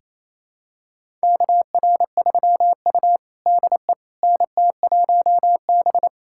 Morse code, heard as a keyed tone, about 27 wpm, 700 Hz. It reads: KR3U DE NT1B
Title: USA Callsign DE Callsign - 4 Characters - 28wpm